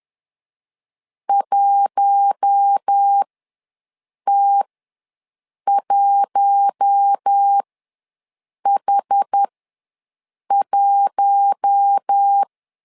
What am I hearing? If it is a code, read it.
Morse code, heard as a keyed tone, about 10 wpm, spaced as for 6 wpm, 800 Hz. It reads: 1T1H1
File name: LEZIONI / 4 Q1 ws 10wpm 2538